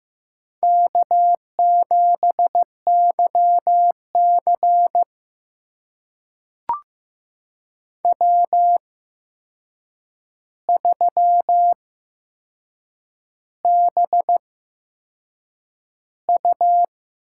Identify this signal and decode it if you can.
Morse code, heard as a keyed tone, about 15 wpm, spaced as for 6 wpm, 700 Hz. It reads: K7YC W3BU
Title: USA Call Signs - 4 Characters - 8x Character Spacing - 15wpm